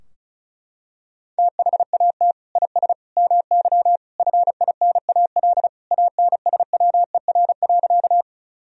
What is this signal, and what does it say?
Morse code, heard as a keyed tone, about 36 wpm, 700 Hz. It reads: THAT IS MY FINAL ANSWER.
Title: Sentences from Top 1000 Words - Review Part 2 - 35wpm